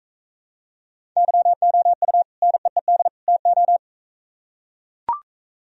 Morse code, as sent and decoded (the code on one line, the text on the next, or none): -.-- --- ..- / -. . -.. / - ---
YOU NEED TO